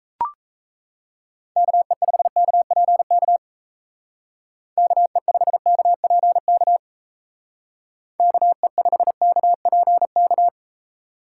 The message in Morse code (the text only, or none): KE5KPK KE5KPK KE5KPK